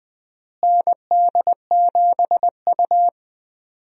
Morse code, heard as a keyed tone, about 19 wpm, 700 Hz. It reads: ND7U
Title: USA Call Signs - 4 Characters - Repeated 3x - 20wpm